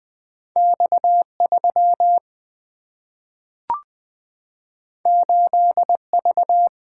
Morse code, heard as a keyed tone, about 20 wpm, 700 Hz. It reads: X 3 8 V
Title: Single-Letter Single-Number 20wpm